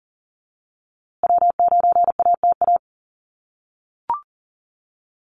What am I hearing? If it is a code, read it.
Morse code, heard as a keyed tone, about 40 wpm, 700 Hz. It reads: W9ATA
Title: USA Call Signs Repeated 3x 40wpm